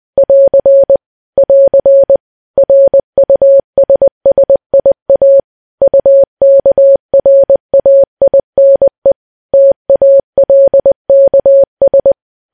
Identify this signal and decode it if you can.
Morse code, heard as a keyed tone, about 20 wpm, 550 Hz. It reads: <AR> <AR> RUSSIA UKRAINE TALKS
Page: news-headlines-in-morse-code-at-20-wpm